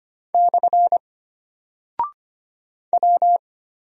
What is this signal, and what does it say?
Morse code, heard as a keyed tone, about 25 wpm, 700 Hz. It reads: / W
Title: Single Character (Letter:Number:Punctuation:Prosign) Alt 25wpm